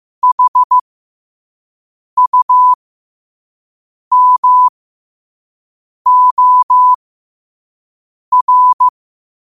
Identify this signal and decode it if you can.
Morse code, heard as a keyed tone, about 15 wpm, spaced as for 5 wpm, 1000 Hz. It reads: HUMOR
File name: qotdfiles / 5wpm